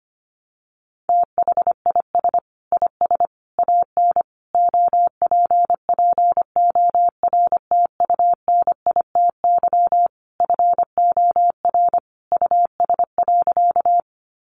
Morse code, heard as a keyed tone, about 25 wpm, 700 Hz. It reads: THIS IS AN OPPORTUNITY FOR US.
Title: Sentences from Top 1000 Words 25wpm